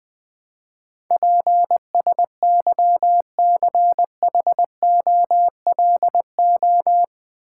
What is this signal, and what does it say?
Morse code, heard as a keyed tone, about 20 wpm, 700 Hz. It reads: PSYCHOLO
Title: Build A Word - 4 Letters At A Time - 12-Letter Words - 20wpm